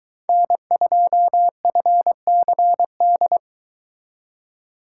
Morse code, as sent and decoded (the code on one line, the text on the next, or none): -. ..--- ..-. -.-. -..
N2FCD